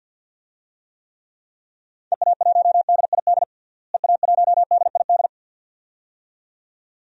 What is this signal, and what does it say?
Morse code, heard as a keyed tone, about 47 wpm, 700 Hz. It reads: EA1DID EA1DID
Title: DXCC-100 Call Signs - 6 Characters - Repeated 2x - 50wpm